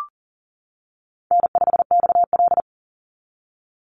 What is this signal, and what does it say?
Morse code, heard as a keyed tone, about 40 wpm, 700 Hz. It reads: N5XL
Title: NA SOTA Chaser Callsigns 40wpm